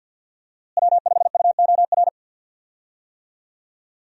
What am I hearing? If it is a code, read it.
Morse code, heard as a keyed tone, about 50 wpm, 700 Hz. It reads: W5UOR
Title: USA Call Signs - 5 Characters - Repeated 3x - 50wpm